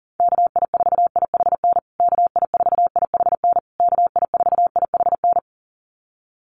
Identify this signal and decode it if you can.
Morse code, heard as a keyed tone, about 40 wpm, 700 Hz. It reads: KI4IHN KI4IHN KI4IHN
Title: USA Call Signs Repeated 3x 40wpm